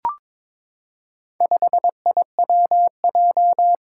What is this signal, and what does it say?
Morse code, heard as a keyed tone, about 22 wpm, 700 Hz. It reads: 5IWJ
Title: Four Challenging Letter-Number Combinations 22wpm